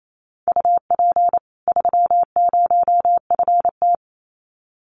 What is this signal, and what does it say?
Morse code, heard as a keyed tone, about 28 wpm, 700 Hz. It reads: UP 30FT
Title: Common QSO Phrases 28wpm